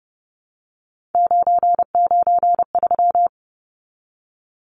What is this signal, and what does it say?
Morse code, heard as a keyed tone, about 30 wpm, 700 Hz. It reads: 993
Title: Numbers - 3-Digit 30wpm